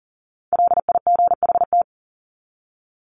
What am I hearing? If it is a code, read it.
Morse code, heard as a keyed tone, about 40 wpm, 700 Hz. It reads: LIGHT
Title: Top 300 Words 40wpm